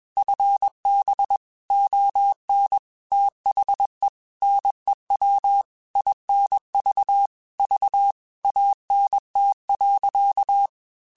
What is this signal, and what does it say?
Morse code, heard as a keyed tone, about 21 wpm, 800 Hz. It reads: FB ON THE NEW INV V ANT.
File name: QSOs / QSOs-mp3s / WD0FBT 1@20